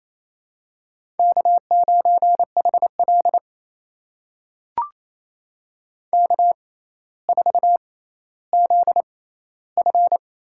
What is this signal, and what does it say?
Morse code, heard as a keyed tone, about 28 wpm, spaced as for 14 wpm, 700 Hz. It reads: K9HL K4ZF
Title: USA Call Signs - 4 Characters - 6x Character Spacing - 28wpm